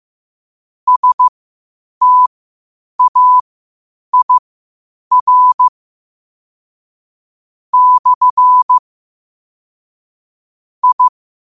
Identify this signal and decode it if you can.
Morse code, heard as a keyed tone, about 15 wpm, spaced as for 8 wpm, 1000 Hz. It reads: STAIR / I
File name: qotdfiles / 8wpm